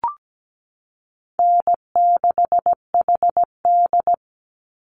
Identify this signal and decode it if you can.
Morse code, heard as a keyed tone, about 17 wpm, 700 Hz. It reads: N6HD
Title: Call Signs Repeated 3x - 17wpm